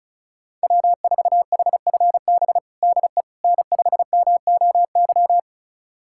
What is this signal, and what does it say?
Morse code, heard as a keyed tone, about 36 wpm, 700 Hz. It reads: W4HFB DE N5MOY
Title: USA Callsign DE Callsign - 5 Characters - 35wpm